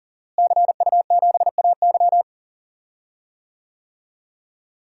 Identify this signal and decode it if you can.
Morse code, heard as a keyed tone, about 40 wpm, 700 Hz. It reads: CU7AY
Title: DXCC-100 Call Signs - 5 Characters - Repeated 3x - 40wpm